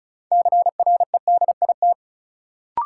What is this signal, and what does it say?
Morse code, heard as a keyed tone, about 36 wpm, 700 Hz. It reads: CREDIT
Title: Sentences from Top 1000 Words - New Word Drill - 35wpm